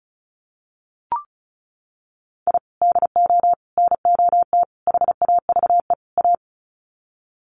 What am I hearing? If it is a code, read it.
Morse code, heard as a keyed tone, about 34 wpm, 700 Hz. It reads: I DO NOT HAVE A
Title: Sets of 5 Words 35wpm